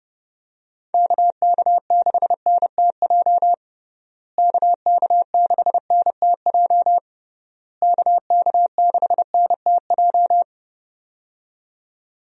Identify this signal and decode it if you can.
Morse code, heard as a keyed tone, about 32 wpm, 700 Hz. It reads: KK6NTJ KK6NTJ KK6NTJ
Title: USA Call Signs - 6 Characters - Repeated 3x - 30wpm